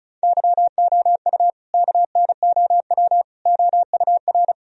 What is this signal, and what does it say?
Morse code, heard as a keyed tone, about 36 wpm, 700 Hz. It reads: YOU KNOW OUR
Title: Sentences from Top 1000 Words - Review Part 4 - 35wpm